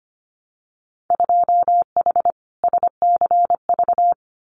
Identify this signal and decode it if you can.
Morse code, heard as a keyed tone, about 25 wpm, 700 Hz. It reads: 2H SCV
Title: Winter Field Day Exchanges Encore 25wpm